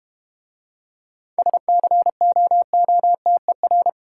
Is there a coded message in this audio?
Yes: Morse code from 1.4 to 3.9 s, heard as a keyed tone, about 32 wpm, 690 Hz. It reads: SCOOTER